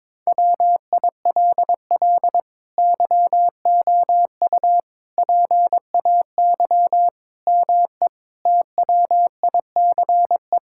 Morse code, heard as a keyed tone, about 22 wpm, 700 Hz. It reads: WILL YOU PAY ME TWICE
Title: Sentences from Top 1600 Words 22wpm